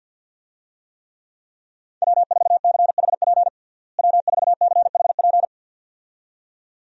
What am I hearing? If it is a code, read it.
Morse code, heard as a keyed tone, about 47 wpm, 700 Hz. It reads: W4CHP W4CHP
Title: USA Call Signs - 5 Characters - Repeated 2x - 50wpm